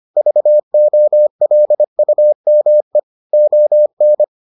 Morse code, heard as a keyed tone, about 25 wpm, 600 Hz. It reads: VOLUME ON